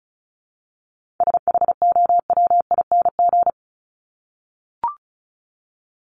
Morse code, heard as a keyed tone, about 34 wpm, 700 Hz. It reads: SHOWING
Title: ING Suffix - Top 100 Words with Base Words 35wpm